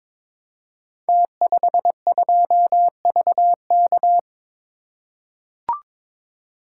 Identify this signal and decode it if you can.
Morse code, heard as a keyed tone, about 23 wpm, 700 Hz. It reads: T52VK
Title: Code Groups - 5 Letter:Numbers - 22wpm